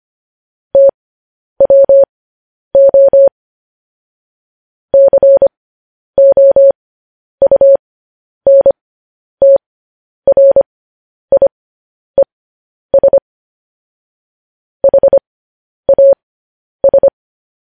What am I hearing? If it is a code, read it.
Morse code, heard as a keyed tone, about 25 wpm, spaced as for 10 wpm, 550 Hz. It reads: TWO COUNTRIES HAS